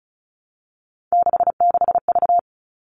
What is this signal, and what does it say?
Morse code, heard as a keyed tone, about 34 wpm, 700 Hz. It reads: 66V
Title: All Cut Numbers - 3-Digit 35wpm